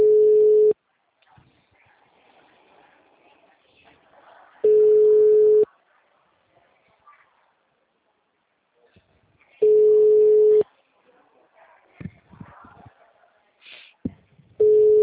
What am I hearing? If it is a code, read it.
Ringback tone (Europe, 425 Hz), with speech.
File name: out- -2017- - - .44